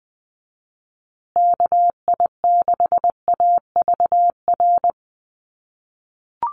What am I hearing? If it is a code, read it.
Morse code, heard as a keyed tone, about 20 wpm, 700 Hz. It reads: KI6AVR
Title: USA Call Signs Repeated 2x 20wpm